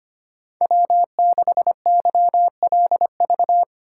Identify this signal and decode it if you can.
Morse code, heard as a keyed tone, about 25 wpm, 700 Hz. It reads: W6YLV
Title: USA Call Signs - 5 Characters - 6x Character Spacing - 25wpm